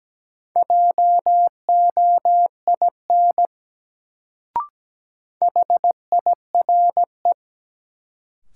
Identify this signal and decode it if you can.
Morse code, heard as a keyed tone, about 17 wpm, 700 Hz. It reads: JOIN HIRE
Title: Sentences from Top 1400 Words - New Word Drill Alt - 17wpm